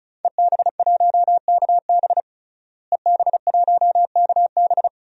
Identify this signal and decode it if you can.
Morse code, heard as a keyed tone, about 36 wpm, 700 Hz. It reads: EB1KB EB1KB
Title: DXCC-100 Call Signs - 5 Characters - Repeated 3x - 35wpm